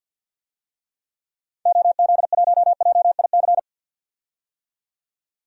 Morse code, heard as a keyed tone, about 50 wpm, 700 Hz. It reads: OZ1JIC
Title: DXCC-100 Call Signs - 6 Characters - 50wpm